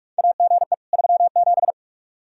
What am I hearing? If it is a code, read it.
Morse code, heard as a keyed tone, about 47 wpm, 700 Hz. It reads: AGE 37
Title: Practice QSO - Exchange 3 - More Info - 45wpm